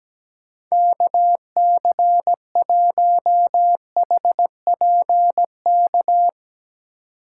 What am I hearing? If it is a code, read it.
Morse code, heard as a keyed tone, about 17 wpm, 700 Hz. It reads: KC1HPK
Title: USA Call Signs - 6 Characters - 2x Character Spacing - 17wpm